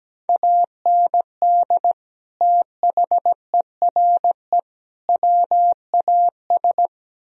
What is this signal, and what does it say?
Morse code, heard as a keyed tone, about 17 wpm, 700 Hz. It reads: AND THERE WAS
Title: Speed-Racing - Sets of 3 Words 17wpm